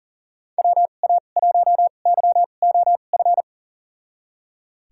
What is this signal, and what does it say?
Morse code, heard as a keyed tone, about 37 wpm, spaced as for 28 wpm, 700 Hz. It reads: WA1YOF
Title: USA Call Signs - 6 Characters - 2x Character Spacing - 40wpm